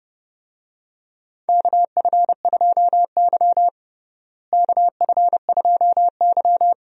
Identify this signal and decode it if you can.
Morse code, heard as a keyed tone, about 30 wpm, 700 Hz. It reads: KF2Y KF2Y
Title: USA Call Signs - 4 Characters - Repeated 2x - 30wpm